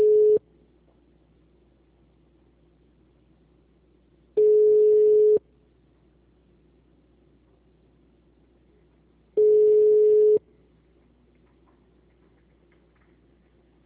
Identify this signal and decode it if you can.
Ringback tone (Europe, 425 Hz).